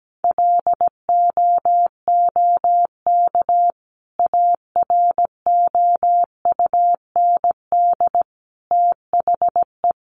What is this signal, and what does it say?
Morse code, heard as a keyed tone, about 17 wpm, 700 Hz. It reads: LOOK AROUND THE
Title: Sentences from Top 600 Words 17wpm